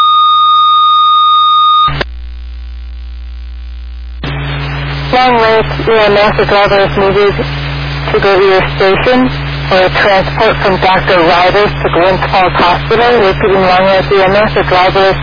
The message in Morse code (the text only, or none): none